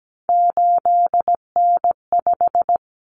Morse code, heard as a keyed tone, about 17 wpm, 700 Hz. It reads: 8N5
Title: Common Cut Numbers - 3-Digit 17wpm